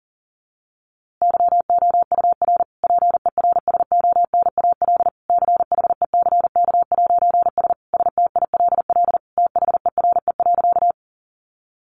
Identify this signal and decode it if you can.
Morse code, heard as a keyed tone, about 40 wpm, 700 Hz. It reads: YOUR PERSONAL CHECK'S STILL THERE.
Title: A Sound Of Thunder 40wpm